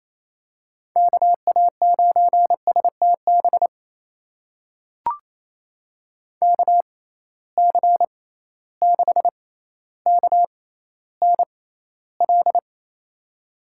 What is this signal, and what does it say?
Morse code, heard as a keyed tone, about 28 wpm, spaced as for 14 wpm, 700 Hz. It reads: KA9STB KC6KNL
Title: USA Call Signs - 6 Characters - 6x Character Spacing - 28wpm